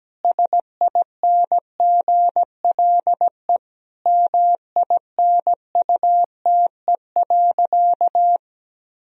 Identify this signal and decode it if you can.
Morse code, heard as a keyed tone, about 17 wpm, 700 Hz. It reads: SINGLE MINUTE.
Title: Sentences from Top 1000 Words - Review Part 4 - 17wpm